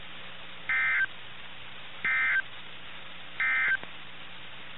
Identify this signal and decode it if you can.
Emergency Alert System (SAME) tones, heard as harsh buzzing bursts. It is heard from 0.6 to 3.8 s.
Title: NOAA Weather Radio - KIH61